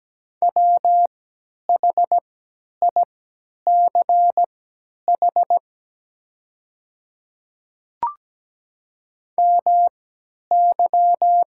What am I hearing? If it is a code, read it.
Morse code, heard as a keyed tone, about 17 wpm, spaced as for 10 wpm, 700 Hz. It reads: WHICH MY
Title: 3x Character Spacing - Top 300 Words - 17wpm